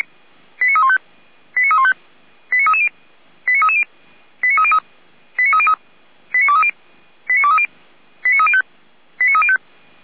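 Five-tone selective call (ZVEI), heard as quick runs of five tones, tones of about 70 ms. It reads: 89315 89315 89330 89330 89303 89303 89230 89230 89305 89305
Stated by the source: Received on SDR.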